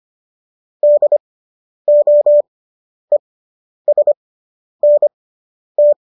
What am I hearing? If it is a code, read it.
Morse code, heard as a keyed tone, about 25 wpm, spaced as for 10 wpm, 590 Hz. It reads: DOESNT